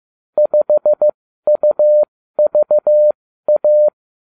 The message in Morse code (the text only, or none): 5UVA